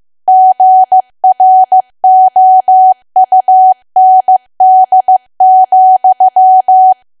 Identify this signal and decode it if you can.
Morse code, heard as a keyed tone, about 15 wpm, 750 Hz. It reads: GROUND,
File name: cwa / W1AW-Level2 / 15WPM